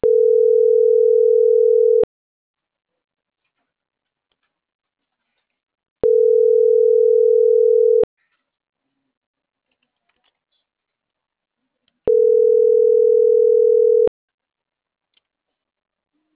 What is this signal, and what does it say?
Ringback tone (North America), 440 + 480 Hz.